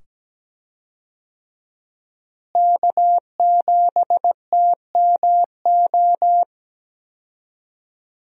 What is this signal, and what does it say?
Morse code, heard as a keyed tone, about 17 wpm, 700 Hz. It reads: K7TMO
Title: USA Call Signs - 5 Characters - Repeated 2x - 17wpm